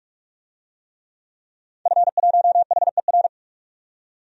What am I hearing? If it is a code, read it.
Morse code, heard as a keyed tone, about 41 wpm, 700 Hz. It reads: F1HER
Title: DXCC-100 Call Signs - 5 Characters - 45wpm